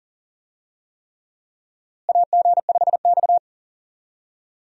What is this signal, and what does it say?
Morse code, heard as a keyed tone, about 40 wpm, 700 Hz. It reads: AG5X
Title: USA Call Signs - 4 Characters - Repeated 3x - 40wpm